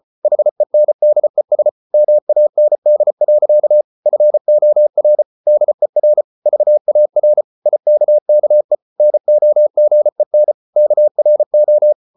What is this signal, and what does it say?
Morse code, heard as a keyed tone, about 34 wpm, 600 Hz. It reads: HENDES MAND. FOR DER VAR IKKE NOGEN KRO
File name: Ferie QRQ 34wpm